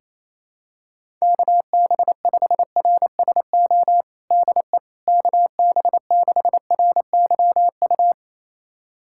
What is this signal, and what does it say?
Morse code, heard as a keyed tone, about 27 wpm, 700 Hz. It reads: KB5RSO DE KB6RYU
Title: USA Callsign DE Callsign - 6 Characters - 28wpm